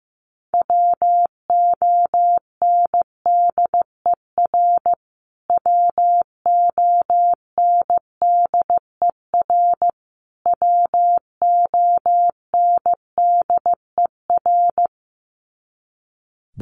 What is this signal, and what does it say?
Morse code, heard as a keyed tone, about 15 wpm, 700 Hz. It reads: WONDER WONDER WONDER
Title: Top 500 Words Repeated 15wpm